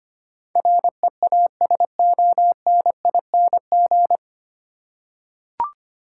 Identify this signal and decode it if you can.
Morse code, heard as a keyed tone, about 25 wpm, 700 Hz. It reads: REASONING
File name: practice / 25wpm / 9 Letter Words - First 3 Letters Given - 25wpm